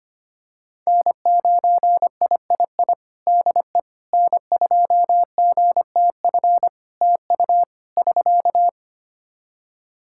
Morse code, heard as a keyed tone, about 25 wpm, 700 Hz. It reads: N9III DE N2GTF TU <SK>
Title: Practice QSO - Exchange 4 - Ending - 25wpm